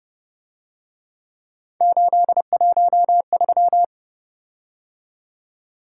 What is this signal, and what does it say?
Morse code, heard as a keyed tone, about 29 wpm, 700 Hz. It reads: 813
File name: Speed-Racing - Three Letter-Number Combinations - 25wpm